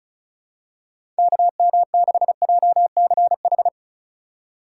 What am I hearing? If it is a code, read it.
Morse code, heard as a keyed tone, about 36 wpm, 700 Hz. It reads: KM6JCH
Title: USA Call Signs - 6 Characters - 4x Character Spacing - 35wpm